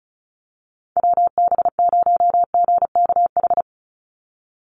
Morse code, heard as a keyed tone, about 34 wpm, 700 Hz. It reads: WB0GKH
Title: Call Signs Repeated 3x - 35wpm